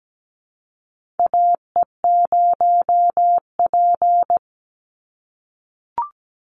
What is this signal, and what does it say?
Morse code, heard as a keyed tone, about 17 wpm, 700 Hz. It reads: AE0P
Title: Call Signs - 17wpm